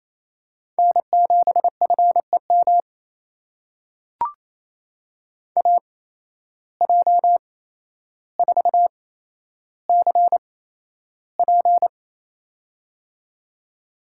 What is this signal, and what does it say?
Morse code, heard as a keyed tone, about 29 wpm, spaced as for 12 wpm, 700 Hz. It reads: N7FEM AJ4CP
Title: USA Call Signs - 5 Characters - 8x Character Spacing - 28wpm